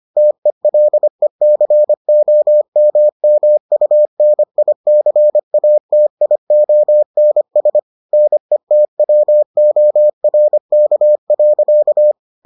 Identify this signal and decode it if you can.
Morse code, heard as a keyed tone, about 25 wpm, 600 Hz. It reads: TELECOMMUNICATIONS NETWORK.